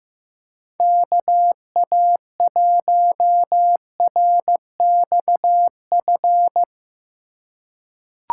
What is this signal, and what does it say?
Morse code, heard as a keyed tone, about 15 wpm, 700 Hz. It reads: KA1RXF
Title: USA Call Signs - 6 Characters - 3x Character Spacing - 15wpm